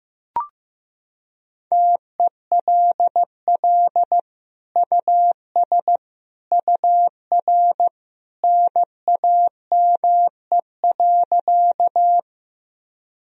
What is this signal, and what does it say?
Morse code, heard as a keyed tone, about 16 wpm, 700 Hz. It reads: TELL US UR NAME.
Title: Sentences from Top 200 Words and Abbreviations 15wpm